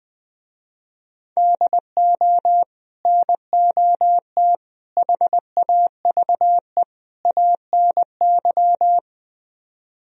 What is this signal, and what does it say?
Morse code, heard as a keyed tone, about 20 wpm, 700 Hz. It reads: DO NOT HAVE ANY
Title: Speed-Racing - Sets of 4 Words 17wpm